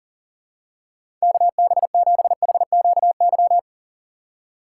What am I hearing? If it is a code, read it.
Morse code, heard as a keyed tone, about 40 wpm, 700 Hz. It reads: KB7HQY
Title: USA Call Signs - 4x Character Spacing - 40wpm